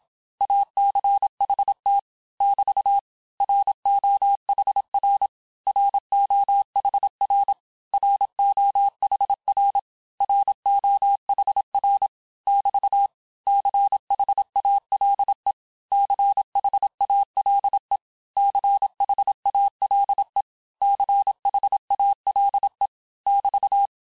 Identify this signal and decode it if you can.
Morse code, heard as a keyed tone, about 27 wpm, 800 Hz. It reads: ACHT <BT> ROHR ROHR ROHR ROHR <BT> CHALE CHALE CHALE CHALE <BT>
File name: cours-du-mercredi / audios / cours-869-texte-LES-MOTS-vitesse-25-MPM